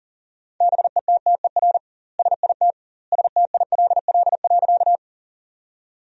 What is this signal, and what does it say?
Morse code, heard as a keyed tone, about 40 wpm, 700 Hz. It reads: BETTER SIT STILL.
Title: 2x Word-Spacing - Sentences from Top 500 Words - 40wpm